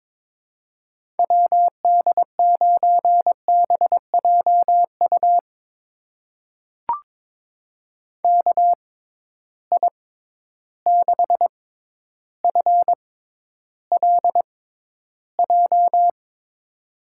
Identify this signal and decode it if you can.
Morse code, heard as a keyed tone, about 22 wpm, spaced as for 11 wpm, 700 Hz. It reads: WD9BJU KI6FLJ
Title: USA Call Signs - 6 Characters - 6x Character Spacing - 22wpm